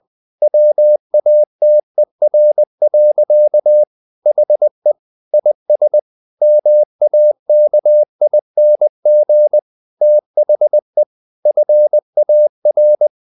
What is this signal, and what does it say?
Morse code, heard as a keyed tone, about 20 wpm, 600 Hz. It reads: WATER. HE IS MAKING THE FAR
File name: the old man and the sea part 2 20wpm